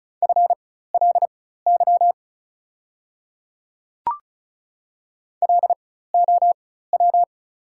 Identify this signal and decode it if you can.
Morse code, heard as a keyed tone, about 34 wpm, spaced as for 16 wpm, 690 Hz. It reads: FLY LOW
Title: 4x Character Spacing - Top 500 Words - 35wpm